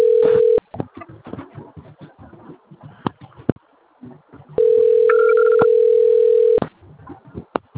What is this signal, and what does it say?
Ringback tone (North America), 440 + 480 Hz.